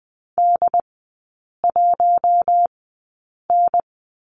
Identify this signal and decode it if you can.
Morse code, heard as a keyed tone, about 20 wpm, spaced as for 8 wpm, 700 Hz. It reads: D1N
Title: 067 Lesson 12e - Letter d - Three Characters 20wpm